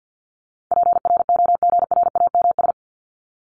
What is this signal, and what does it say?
Morse code, heard as a keyed tone, about 50 wpm, 700 Hz. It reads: PROGRAMS
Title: ES:S Suffix - Top 100 Words with Base Words 50wpm